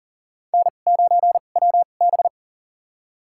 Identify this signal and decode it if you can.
Morse code, heard as a keyed tone, about 40 wpm, spaced as for 29 wpm, 700 Hz. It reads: N9WB
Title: USA Call Signs - 4 Characters - 2x Character Spacing - 40wpm